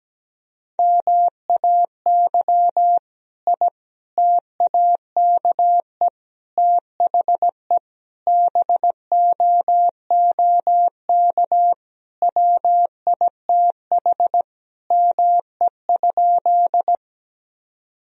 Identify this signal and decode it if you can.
Morse code, heard as a keyed tone, about 17 wpm, 700 Hz. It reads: MAY I TAKE THE BOOK WITH ME?
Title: Sentences from Top 1000 Words - Review Part 1 - 17wpm